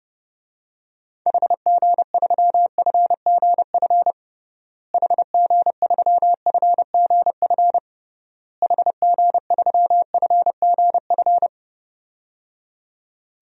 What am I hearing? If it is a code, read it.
Morse code, heard as a keyed tone, about 30 wpm, 700 Hz. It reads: HG3FGF HG3FGF HG3FGF
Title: DXCC-100 Call Signs - 6 Characters - Repeated 3x - 30wpm